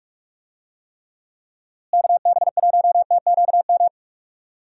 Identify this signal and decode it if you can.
Morse code, heard as a keyed tone, about 46 wpm, 700 Hz. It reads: KB1TQM
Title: USA Call Signs - 6 Characters - Repeated 3x - 45wpm